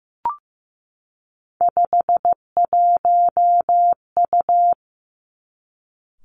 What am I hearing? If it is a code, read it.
Morse code, heard as a keyed tone, about 15 wpm, 700 Hz. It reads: 51U
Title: Three Challenging Letter-Number Combinations 15wpm